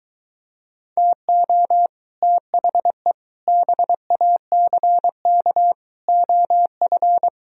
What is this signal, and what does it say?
Morse code, heard as a keyed tone, about 23 wpm, 700 Hz. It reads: TO THE BACK OF